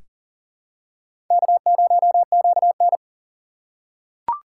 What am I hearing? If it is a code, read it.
Morse code, heard as a keyed tone, about 40 wpm, 700 Hz. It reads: K0QN
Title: USA Call Signs - 4 Characters - 3x Character Spacing - 40wpm